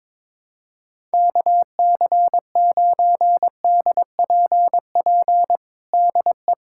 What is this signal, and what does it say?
Morse code, heard as a keyed tone, about 22 wpm, 700 Hz. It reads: KC9DPP DE
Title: USA Callsign DE Callsign - 6 Characters - 22wpm